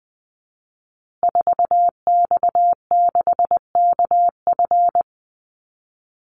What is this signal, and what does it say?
Morse code, heard as a keyed tone, about 19 wpm, 700 Hz. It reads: 4X6KF
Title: Call Signs Repeated 2x - 20wpm